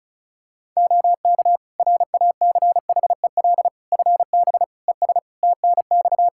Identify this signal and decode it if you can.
Morse code, heard as a keyed tone, about 36 wpm, 700 Hz. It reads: OK RACHEL FB ES TNX
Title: Practice QSO - Exchange 2 - Info - 35wpm